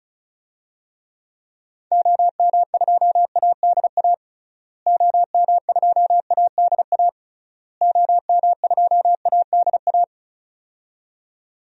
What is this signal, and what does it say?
Morse code, heard as a keyed tone, about 34 wpm, 700 Hz. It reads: OM2ADA OM2ADA OM2ADA
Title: DXCC-100 Call Signs - 6 Characters - Repeated 3x - 35wpm